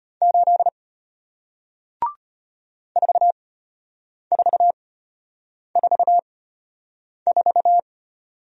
Morse code, heard as a keyed tone, about 34 wpm, spaced as for 8 wpm, 700 Hz. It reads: 8 4444